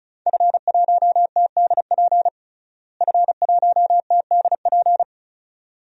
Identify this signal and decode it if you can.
Morse code, heard as a keyed tone, about 34 wpm, 700 Hz. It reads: F1TDP F1TDP
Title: DXCC-100 Call Signs - 5 Characters - Repeated 2x - 35wpm